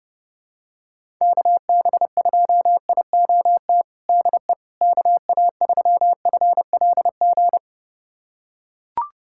Morse code, heard as a keyed tone, about 32 wpm, 700 Hz. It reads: KB2IOT DE KA3FLG
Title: USA Callsign DE Callsign - 6 Characters - 30wpm